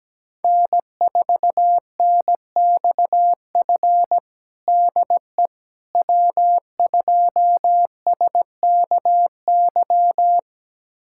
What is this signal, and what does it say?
Morse code, heard as a keyed tone, about 17 wpm, 700 Hz. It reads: N4NXF DE W2SKY
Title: USA Callsign DE Callsign - 5 Characters - 17wpm